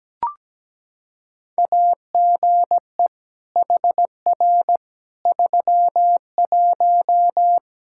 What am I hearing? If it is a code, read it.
Morse code, heard as a keyed tone, about 17 wpm, 700 Hz. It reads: AGE HR 31
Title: Practice QSO - Exchange 3 - More Info - 17wpm